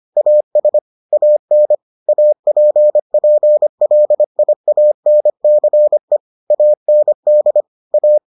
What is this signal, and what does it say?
Morse code, heard as a keyed tone, about 25 wpm, 600 Hz. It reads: AS AN APPLIANCE AND A